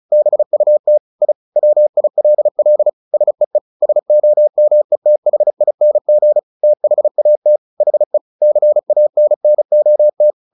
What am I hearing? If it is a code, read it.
Morse code, heard as a keyed tone, about 34 wpm, 600 Hz. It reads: BUT I WILL SEE SOMETHING THAT HE CANNOT
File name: the old man and the sea / the old man and the sea 1 35wpm